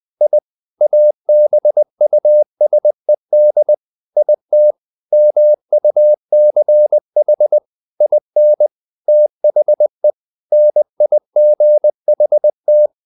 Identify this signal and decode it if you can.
Morse code, heard as a keyed tone, about 20 wpm, 600 Hz. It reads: I ABUSED IT MUCH IN THE NIGHT
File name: the old man and the sea / the old man and the sea part 1 20wpm